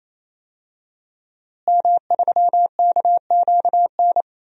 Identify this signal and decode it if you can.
Morse code, heard as a keyed tone, about 27 wpm, 700 Hz. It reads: M3KQN
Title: DXCC-100 Call Signs - 5 Characters - Repeated 3x - 28wpm